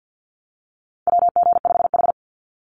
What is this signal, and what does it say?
Morse code, heard as a keyed tone, about 49 wpm, 700 Hz. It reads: WG5H
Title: Call Signs Repeated 3x - 50wpm